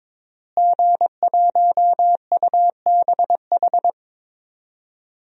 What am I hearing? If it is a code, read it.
Morse code, heard as a keyed tone, about 21 wpm, 700 Hz. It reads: G1UBH
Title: DXCC-100 Call Signs - 5 Characters - Repeated 3x - 22wpm